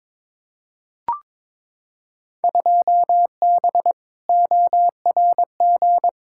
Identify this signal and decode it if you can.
Morse code, heard as a keyed tone, about 22 wpm, 700 Hz. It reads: 2B ORG
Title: Field Day Exchanges Encore - 22wpm